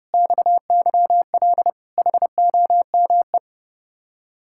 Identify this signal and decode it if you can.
Morse code, heard as a keyed tone, about 31 wpm, 700 Hz. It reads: XYL HOME